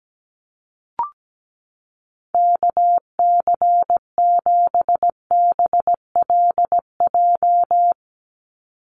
Morse code, heard as a keyed tone, about 17 wpm, 700 Hz. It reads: KC7BLJ
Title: USA Call Signs 17wpm